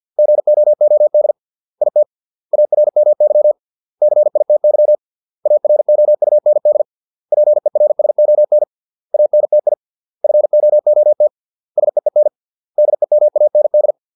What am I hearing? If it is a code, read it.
Morse code, heard as a keyed tone, about 50 wpm, 600 Hz. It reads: GOOD IT ARMY CITY AROUND PERSON ANTI FOOT SEEN DEMAND